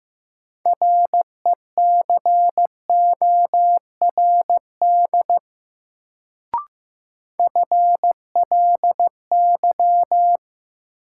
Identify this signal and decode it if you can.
Morse code, heard as a keyed tone, about 15 wpm, 700 Hz. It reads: RECORD FLY
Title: Sentences from Top 500 Words - New Word Drill - 15wpm